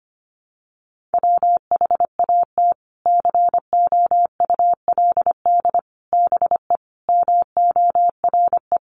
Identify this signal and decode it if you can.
Morse code, heard as a keyed tone, about 25 wpm, 700 Hz. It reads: WHAT COULD BE MORE
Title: Sentences from Top 1000 Words 25wpm